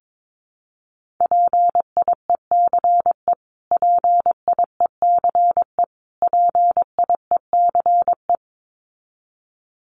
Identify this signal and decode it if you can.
Morse code, heard as a keyed tone, about 22 wpm, 700 Hz. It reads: PIECE PIECE PIECE
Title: Top 500 Words Repeated 22wpm